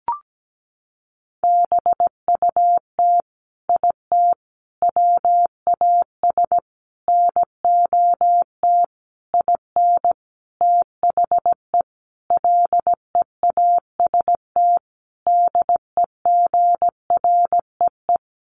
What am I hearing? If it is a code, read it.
Morse code, heard as a keyed tone, about 17 wpm, 700 Hz. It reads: BUT IT WAS NOT IN THE LEAST DEGREE
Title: Sentences from Top 1500 Words 17wpm